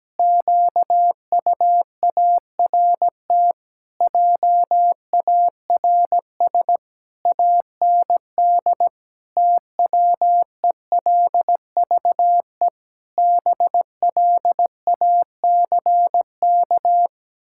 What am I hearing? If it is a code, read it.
Morse code, heard as a keyed tone, about 17 wpm, 700 Hz. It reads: QUART JARS AND TWELVE BLACK
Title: Pangram Challenge v2 - 17wpm